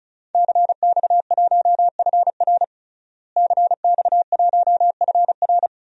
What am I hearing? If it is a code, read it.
Morse code, heard as a keyed tone, about 36 wpm, 700 Hz. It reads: CX1FR CX1FR
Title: DXCC-100 Call Signs - 5 Characters - Repeated 2x - 35wpm